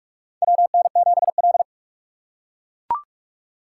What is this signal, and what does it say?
Morse code, heard as a keyed tone, about 42 wpm, 700 Hz. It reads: WN7L
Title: USA Call Signs - 4 Characters - 2x Character Spacing - 45wpm